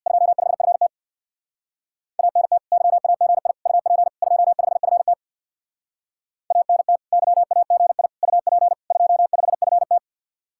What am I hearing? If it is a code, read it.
Morse code, heard as a keyed tone, about 66 wpm, 700 Hz. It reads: 25FT ANT YAGI UP 25FT ANT YAGI UP 25FT